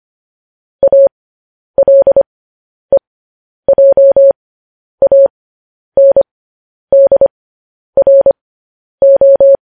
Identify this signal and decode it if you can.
Morse code, heard as a keyed tone, about 25 wpm, spaced as for 10 wpm, 550 Hz. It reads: ALEJANDRO